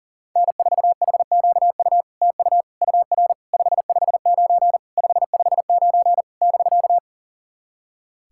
Morse code, heard as a keyed tone, about 40 wpm, 700 Hz. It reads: N4HQU TU UR 559 559 <BK>
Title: SOTA QSO Practice 40wpm